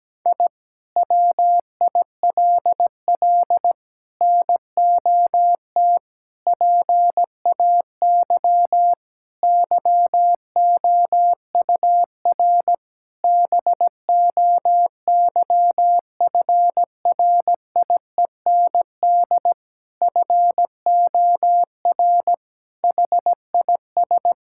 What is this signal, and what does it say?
Morse code, heard as a keyed tone, about 17 wpm, 700 Hz. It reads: I WILL NOT PAY YOUR BOYFRIEND FOR HIS
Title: Sentences from Top 1000 Words - Review Part 5 - 17wpm